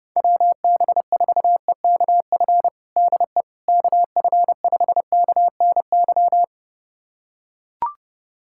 Morse code, heard as a keyed tone, about 29 wpm, 700 Hz. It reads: WB4EKF DE KF5KNY
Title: USA Callsign DE Callsign - 6 Characters - 30wpm